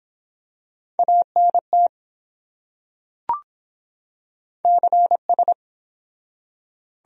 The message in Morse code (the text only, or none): ANT CS